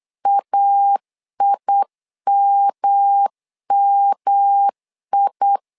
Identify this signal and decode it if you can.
Morse code, heard as a keyed tone, about 8 wpm, 800 Hz. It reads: AIMMI
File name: LEZIONI / 3 MWRO / 3 M 8wpm 2300